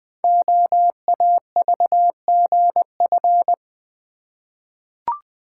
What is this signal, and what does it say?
Morse code, heard as a keyed tone, about 20 wpm, 700 Hz. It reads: OAVGF
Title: Code Groups - 5 Letters - 20wpm